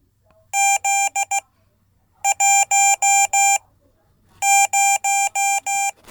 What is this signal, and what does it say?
Morse code, heard as a keyed tone, about 16 wpm, spaced as for 8 wpm, 790 Hz. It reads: Z10
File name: Code / 5min-b